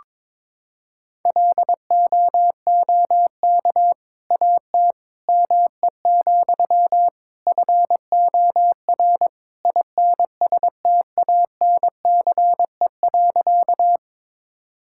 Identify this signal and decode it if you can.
Morse code, heard as a keyed tone, about 22 wpm, 700 Hz. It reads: LOOK AT ME, FOR INSTANCE.
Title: Sentences from Top 1000 Words - Review Part 5 - 22wpm